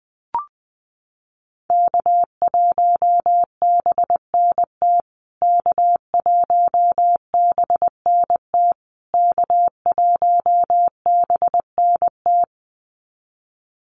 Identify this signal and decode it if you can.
Morse code, heard as a keyed tone, about 20 wpm, 700 Hz. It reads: K1BNT K1BNT K1BNT
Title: USA Call Signs Repeated 3x 20wpm